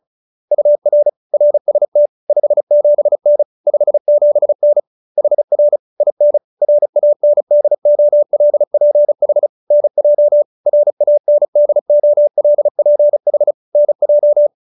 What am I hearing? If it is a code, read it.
Morse code, heard as a keyed tone, about 34 wpm, 590 Hz. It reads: UR RST 57N 57N HR IN RANDOLPH NJ RANDOLPH NJ